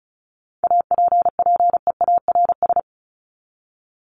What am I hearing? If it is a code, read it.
Morse code, heard as a keyed tone, about 34 wpm, 700 Hz. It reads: APPEARS
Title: ES:S Suffix - Top 100 Words with Base Words 35wpm